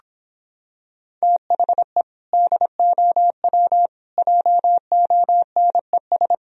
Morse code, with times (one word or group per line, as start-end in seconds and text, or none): 1.2-2.0 THE
2.3-3.9 DOW
4.2-6.3 JONES